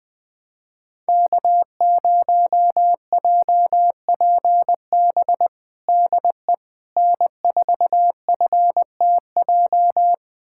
Morse code, heard as a keyed tone, about 21 wpm, 700 Hz. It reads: K0JPB DE N4FTJ
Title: USA Callsign DE Callsign - 5 Characters - 20wpm